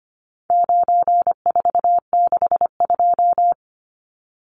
Morse code, heard as a keyed tone, about 25 wpm, 700 Hz. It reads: 9462
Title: Numbers - 4-Digit 25wpm